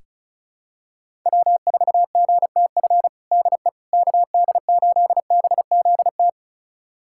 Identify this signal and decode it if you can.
Morse code, heard as a keyed tone, about 36 wpm, 700 Hz. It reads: W4GTF DE KD8BZT
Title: USA Callsign DE Callsign 35wpm